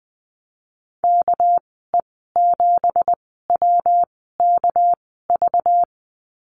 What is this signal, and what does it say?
Morse code, heard as a keyed tone, about 20 wpm, spaced as for 15 wpm, 700 Hz. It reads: KE7WKV
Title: 245 Lesson 39g - Letter z - Call Signs 20wpm